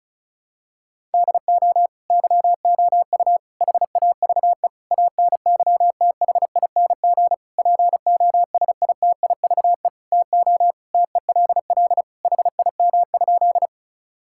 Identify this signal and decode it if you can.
Morse code, heard as a keyed tone, about 34 wpm, 700 Hz. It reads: DO YOU HAVE ANYTHING POSITIVE TO TELL HIM?
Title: Sentences from Top 1000 Words - Review Part 5 - 35wpm